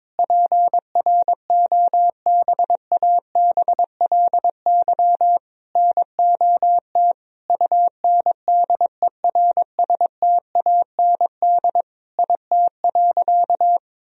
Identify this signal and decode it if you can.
Morse code, heard as a keyed tone, about 22 wpm, 700 Hz. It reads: PROBABLY NOT UNDERSTAND IT.
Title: Sentences from Top 1000 Words - Review Part 4 - 22wpm